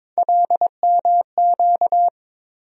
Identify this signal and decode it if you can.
Morse code, heard as a keyed tone, about 22 wpm, 700 Hz. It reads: LMQ
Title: USA Call Sign Postfixes - 3 Letters 22wpm